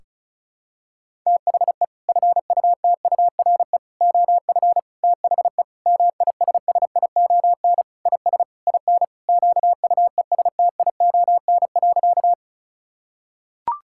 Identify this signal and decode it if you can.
Morse code, heard as a keyed tone, about 36 wpm, 700 Hz. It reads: THE FUTURE OF THE MISSION IS IN QUESTION.
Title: Sentences from Top 1600 Words 35wpm